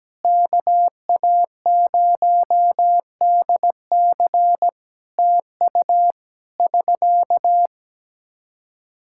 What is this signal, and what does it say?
Morse code, heard as a keyed tone, about 17 wpm, 700 Hz. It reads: KA0DC TU <SK>
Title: Practice QSO - Exchange 4 - Ending - 17wpm